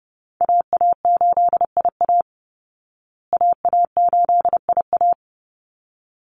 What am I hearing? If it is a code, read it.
Morse code, heard as a keyed tone, about 30 wpm, 700 Hz. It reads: AA8IA AA8IA
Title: Call Signs Repeated 3x - 30wpm